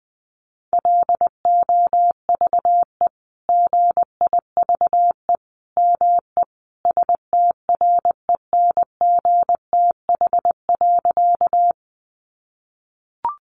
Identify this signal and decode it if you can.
Morse code, heard as a keyed tone, about 20 wpm, 700 Hz. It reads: LOVE GIVE ME STRENGTH.
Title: Sentences from Top 1300 Words 20wpm